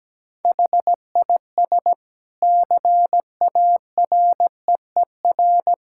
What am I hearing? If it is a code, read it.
Morse code, heard as a keyed tone, about 17 wpm, 700 Hz. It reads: HIS CAREER
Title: Sentences from Top 1000 Words - Review Part 6 - 17wpm